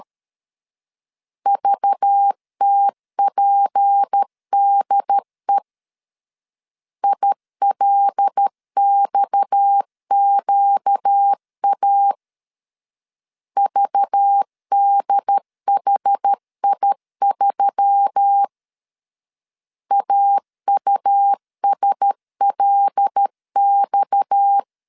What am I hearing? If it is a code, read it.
Morse code, heard as a keyed tone, about 13 wpm, 800 Hz. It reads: VTPDE ILXQA VDHI3 AUSLX
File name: LEZIONI / 6 PKX0 / 6 PKX 12wpm 2024